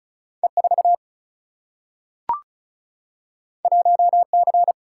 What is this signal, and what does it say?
Morse code, heard as a keyed tone, about 36 wpm, 700 Hz. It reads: E 4 1 C
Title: Single-Letter Single-Number 35wpm